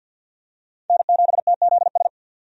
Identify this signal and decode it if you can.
Morse code, heard as a keyed tone, about 51 wpm, 700 Hz. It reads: N7TZS
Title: USA Call Signs - 5 Characters - 50wpm